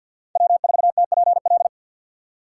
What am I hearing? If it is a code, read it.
Morse code, heard as a keyed tone, about 52 wpm, 700 Hz. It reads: W4TPL
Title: USA Call Signs - 5 Characters - 3x Character Spacing - 50wpm